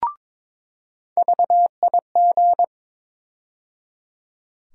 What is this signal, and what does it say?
Morse code, heard as a keyed tone, about 22 wpm, 700 Hz. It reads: VIG
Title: USA Call Sign Postfixes - 3 Letters 22wpm